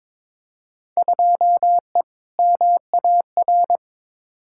Morse code, heard as a keyed tone, about 22 wpm, 700 Hz. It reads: 2E MAR
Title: Field Day Exchanges Encore - 22wpm